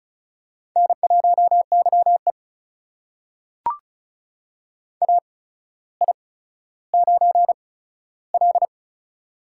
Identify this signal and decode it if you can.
Morse code, heard as a keyed tone, about 36 wpm, spaced as for 15 wpm, 700 Hz. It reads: N1YE AI9L
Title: USA Call Signs - 4 Characters - 8x Character Spacing - 35wpm